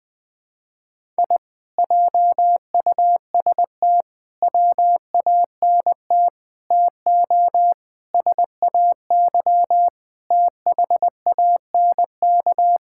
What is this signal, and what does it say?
Morse code, heard as a keyed tone, about 20 wpm, 700 Hz. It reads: I JUST WANT TO SAY THANK
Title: Sentences with 2-Word Phrases - 20wpm